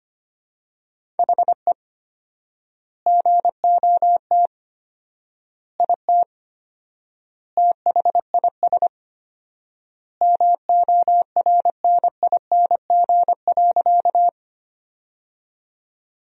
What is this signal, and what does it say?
Morse code, heard as a keyed tone, about 25 wpm, 700 Hz. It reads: HE GOT IT THIS MORNING.
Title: 4x Word-Spacing - Sentences from Top 400 Words - 25wpm